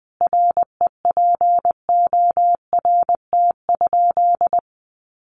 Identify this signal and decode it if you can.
Morse code, heard as a keyed tone, about 20 wpm, 700 Hz. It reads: REPORT?
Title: Sentences from Top 900 Words 20wpm